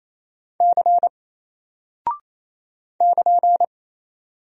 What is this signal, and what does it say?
Morse code, heard as a keyed tone, about 27 wpm, 700 Hz. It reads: C <KN>
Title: Single Character (Letter:Number:Punctuation:Prosign) 28wpm